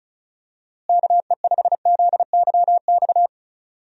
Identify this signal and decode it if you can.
Morse code, heard as a keyed tone, about 36 wpm, 700 Hz. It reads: KE5ZYX
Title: USA Call Signs - 6 Characters - Repeated 2x - 35wpm